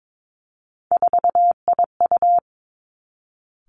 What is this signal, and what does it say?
Morse code, heard as a keyed tone, about 22 wpm, 700 Hz. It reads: 4IU
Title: Three Challenging Letter-Number Combinations 22wpm